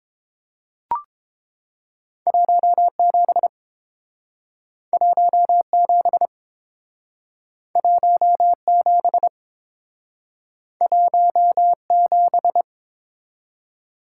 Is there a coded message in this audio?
Yes: Morse code from 2.3 to 12.6 s, heard as a keyed tone, about 29 wpm, 700 Hz. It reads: 17 17 17 17